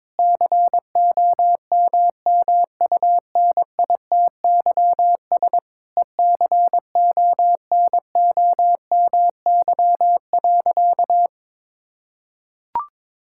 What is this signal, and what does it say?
Morse code, heard as a keyed tone, about 21 wpm, 700 Hz. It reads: COMMUNITYS ECONOMY.
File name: Difficult Sentences with Hints - 22wpm